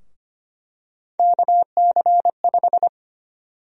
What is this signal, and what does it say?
Morse code, heard as a keyed tone, about 25 wpm, 700 Hz. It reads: KC5
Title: USA Call Sign Prefixes - 25wpm